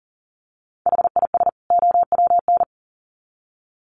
Morse code, heard as a keyed tone, about 40 wpm, 700 Hz. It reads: HIS OWN
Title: Sets of 2 Words 40wpm